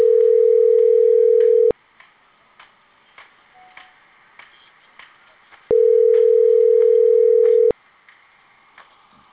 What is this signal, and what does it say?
Ringback tone (North America), 440 + 480 Hz.